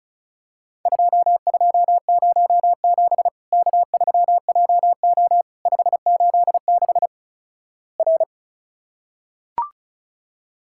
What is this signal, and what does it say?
Morse code, heard as a keyed tone, about 36 wpm, 700 Hz. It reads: 2207 K3JO 586 R
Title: Worked All Europe DX Contest - QTC Exchange - 35wpm